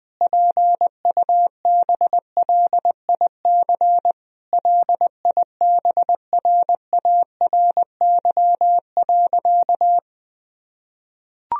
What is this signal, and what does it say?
Morse code, heard as a keyed tone, about 20 wpm, 700 Hz. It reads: PUBLIC LIBRARY.
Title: Sentences from Top 1000 Words - Review Part 4 - 20wpm